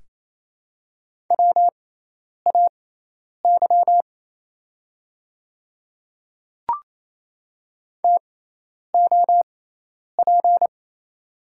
Morse code, heard as a keyed tone, about 28 wpm, spaced as for 10 wpm, 700 Hz. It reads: WAY TOP
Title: 6x Character Spacing - Top 500 Words - 28wpm